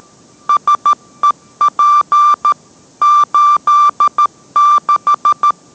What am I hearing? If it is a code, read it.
Morse code, heard as a keyed tone, about 13 wpm, 1180 Hz. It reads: SEP 86